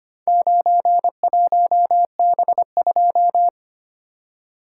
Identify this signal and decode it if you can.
Morse code, heard as a keyed tone, about 25 wpm, 700 Hz. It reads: 91B2
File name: practice / 25wpm / Four Letter-Number Combinations - 25wpm